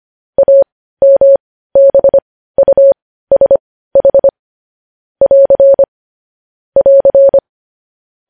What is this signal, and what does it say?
Morse code, heard as a keyed tone, about 25 wpm, spaced as for 15 wpm, 550 Hz. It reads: AMBUSH <AR> <AR>
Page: news-headlines-in-morse-code-at-15-wpm